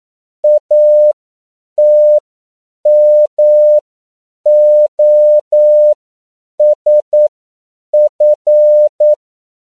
Morse code, heard as a keyed tone, about 9 wpm, spaced as for 7 wpm, 600 Hz. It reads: ATMOSF
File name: CW / 10PPM